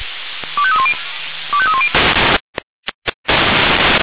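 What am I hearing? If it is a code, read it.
Five-tone selective call (ZVEI), heard as quick runs of five tones, tones of about 70 ms. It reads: 25310 25310
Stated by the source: ZVEI five-tone call received by SDR.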